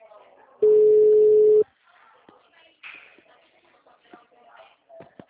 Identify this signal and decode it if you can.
Ringback tone (Europe, 425 Hz), with speech.
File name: out- -2005- - - .5148